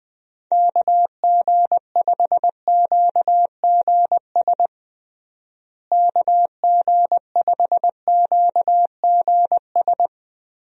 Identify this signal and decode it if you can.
Morse code, heard as a keyed tone, about 20 wpm, 700 Hz. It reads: KG5QGS KG5QGS
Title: USA Call Signs - 6 Characters - Repeated 3x - 20wpm